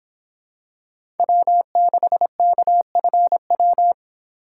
Morse code, heard as a keyed tone, about 25 wpm, 700 Hz. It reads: W6KFW